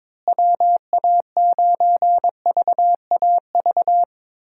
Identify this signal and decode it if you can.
Morse code, heard as a keyed tone, about 21 wpm, 700 Hz. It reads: WA9VAV
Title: USA Call Signs - 6 Characters - Repeated 3x - 22wpm